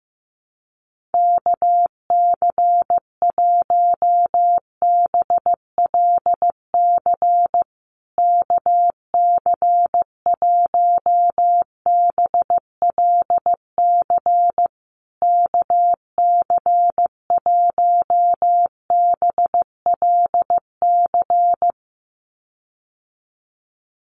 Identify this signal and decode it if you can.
Morse code, heard as a keyed tone, about 14 wpm, 700 Hz. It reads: KC1BLC KC1BLC KC1BLC
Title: USA Call Signs Repeated 3x 15wpm